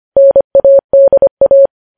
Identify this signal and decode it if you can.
Morse code, heard as a keyed tone, about 25 wpm, 550 Hz. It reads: NADA
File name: podcast / News headlines-1x-25x25